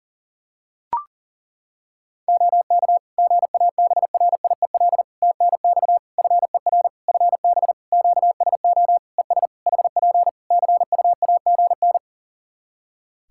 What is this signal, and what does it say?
Morse code, heard as a keyed tone, about 40 wpm, 700 Hz. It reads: OK GABRIEL TNX FER FB QSO ES HP CUAGN
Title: Practice QSO - Exchange 4 - Ending - 40wpm